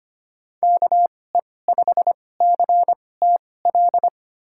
Morse code, heard as a keyed tone, about 25 wpm, spaced as for 18 wpm, 700 Hz. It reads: KE5CTL
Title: USA Call Signs - 6 Characters - 2x Character Spacing - 25wpm